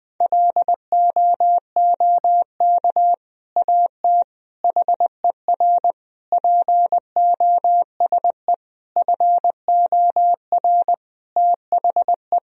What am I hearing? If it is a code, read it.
Morse code, heard as a keyed tone, about 20 wpm, 700 Hz. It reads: LOOK AT HER POSE FOR THE
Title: Sentences from Top 1000 Words - Review Part 2 - 20wpm